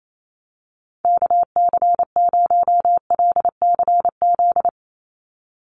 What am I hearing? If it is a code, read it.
Morse code, heard as a keyed tone, about 28 wpm, 700 Hz. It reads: KC0LCZ
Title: USA Call Signs 28wpm